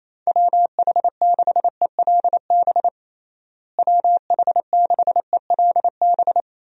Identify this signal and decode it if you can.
Morse code, heard as a keyed tone, about 28 wpm, 700 Hz. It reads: WH6ELB WH6ELB
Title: USA Call Signs - 6 Characters - Repeated 3x - 28wpm